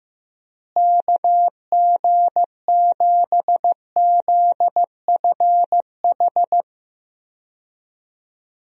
Morse code, heard as a keyed tone, about 15 wpm, 700 Hz. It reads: KG7ZFH
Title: USA Call Signs - 6 Characters - Repeated 2x - 15wpm